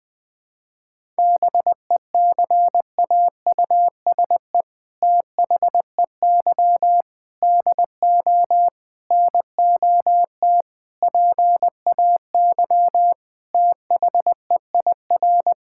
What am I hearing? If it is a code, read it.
Morse code, heard as a keyed tone, about 21 wpm, 700 Hz. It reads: BECAUSE THEY DO NOT PAY THEIR
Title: Sentences from Top 1000 Words - Review Part 5 - 20wpm